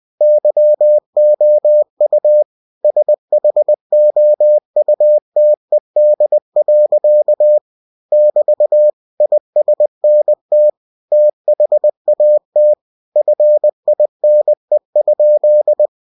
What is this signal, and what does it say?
Morse code, heard as a keyed tone, about 20 wpm, 600 Hz. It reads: YOU SHOUTED. <BT> ISNT THAT FINE?
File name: cw-books / winnie-the-pooh / winnie-20wpm-page-0006